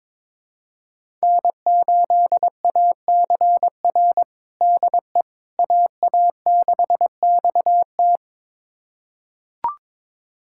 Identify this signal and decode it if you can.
Morse code, heard as a keyed tone, about 22 wpm, 700 Hz. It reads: N8ACR DE AA6XT
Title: USA Callsign DE Callsign 22wpm